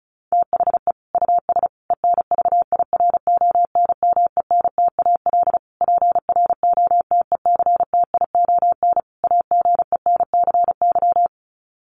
Morse code, heard as a keyed tone, about 34 wpm, 700 Hz. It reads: THE US ENVIRONMENTAL PROTECTION AGENCY
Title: Sets of 5 Words Encore2 35wpm